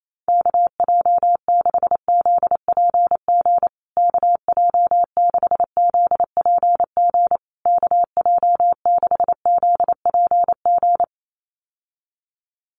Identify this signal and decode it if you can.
Morse code, heard as a keyed tone, about 28 wpm, 700 Hz. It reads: KJ6ZPG KJ6ZPG KJ6ZPG
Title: USA Call Signs Repeated 3x 28wpm